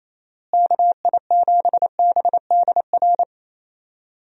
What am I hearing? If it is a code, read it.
Morse code, heard as a keyed tone, about 28 wpm, 700 Hz. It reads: KI7BDR
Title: USA Call Signs - 6 Characters - Repeated 3x - 28wpm